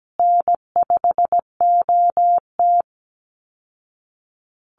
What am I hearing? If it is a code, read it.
Morse code, heard as a keyed tone, about 17 wpm, 700 Hz. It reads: N5OT
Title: Easy Call Signs Repeated 3x - 17wpm